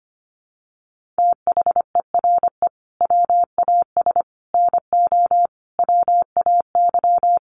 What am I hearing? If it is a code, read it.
Morse code, heard as a keyed tone, about 25 wpm, 700 Hz. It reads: THERE WAS NO WAY
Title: Sets of 5 Words Encore 25wpm